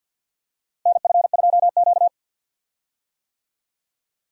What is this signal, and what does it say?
Morse code, heard as a keyed tone, about 52 wpm, 700 Hz. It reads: NF2Q
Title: USA Call Signs - 4 Characters - Repeated 3x - 50wpm